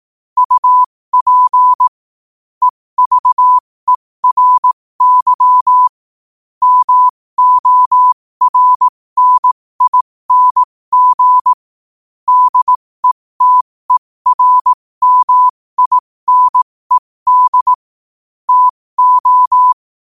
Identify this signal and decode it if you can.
Morse code, heard as a keyed tone, about 18 wpm, spaced as for 15 wpm, 1000 Hz. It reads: UP EVERY MORNING DETERMINED TO
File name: qotdfiles / 15wpm